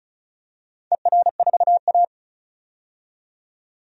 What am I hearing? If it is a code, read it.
Morse code, heard as a keyed tone, about 34 wpm, 700 Hz. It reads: ER4A
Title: DXCC-100 Call Signs - 4 Characters - 35wpm